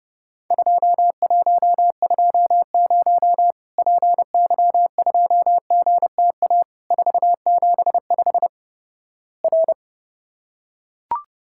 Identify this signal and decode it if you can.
Morse code, heard as a keyed tone, about 30 wpm, 700 Hz. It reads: 2120 PY2GTA 475 R
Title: Worked All Europe DX Contest - QTC Exchange - 30wpm